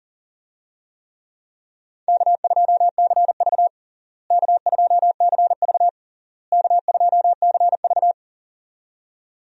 Morse code, heard as a keyed tone, about 40 wpm, 700 Hz. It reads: K2CV K2CV K2CV
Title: USA Call Signs - 4 Characters - Repeated 3x - 40wpm